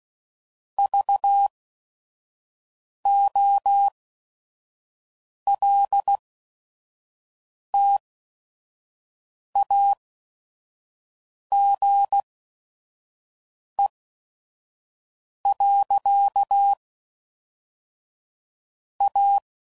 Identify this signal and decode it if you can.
Morse code, heard as a keyed tone, about 16 wpm, spaced as for 5 wpm, 800 Hz. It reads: VOLTAGE. A
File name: CW Code Groups @5WPM / Technical Subjects / XC